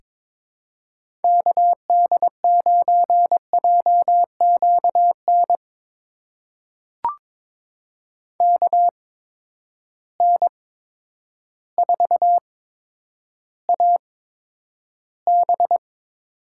Morse code, heard as a keyed tone, about 22 wpm, spaced as for 10 wpm, 700 Hz. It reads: KD9JQN KN4AB
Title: USA Call Signs - 6 Characters - 8x Character Spacing - 22wpm